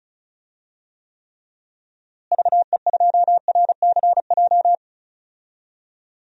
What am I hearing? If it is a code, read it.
Morse code, heard as a keyed tone, about 34 wpm, 700 Hz. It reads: VE2RCJ
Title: DXCC-100 Call Signs - 6 Characters - Repeated 2x - 35wpm